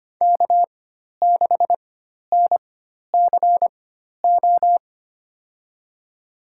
Morse code, heard as a keyed tone, about 25 wpm, spaced as for 12 wpm, 700 Hz. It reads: K6NCO
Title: USA Call Signs - 5 Characters - 4x Character Spacing - 25wpm